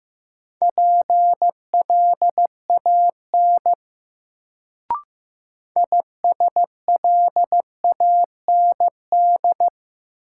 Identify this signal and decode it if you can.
Morse code, heard as a keyed tone, about 15 wpm, 700 Hz. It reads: PLAN ISLAND
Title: Sentences from Top 500 Words - New Word Drill - 15wpm